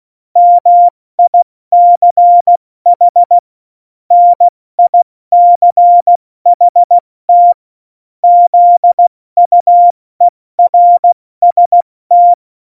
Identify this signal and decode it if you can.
Morse code, heard as a keyed tone, about 16 wpm, 700 Hz. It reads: MICH NICHT ZUERST
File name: Morsekurs / Witze / Witz 09. Juli 2019, 70 F80